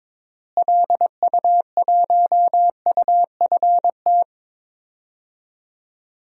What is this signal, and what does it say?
Morse code, heard as a keyed tone, about 22 wpm, 700 Hz. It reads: LU1UFT
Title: DXCC-100 Call Signs - 6 Characters - Repeated 2x - 22wpm